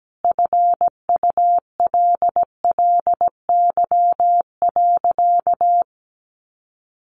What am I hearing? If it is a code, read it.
Morse code, heard as a keyed tone, about 17 wpm, 700 Hz. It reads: FULLY.
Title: Sentences from Top 1300 Words 17wpm